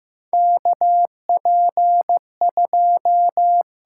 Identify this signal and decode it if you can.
Morse code, heard as a keyed tone, about 15 wpm, 700 Hz. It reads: KP2
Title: USA Call Signs - 4 Characters - 8x Character Spacing - 15wpm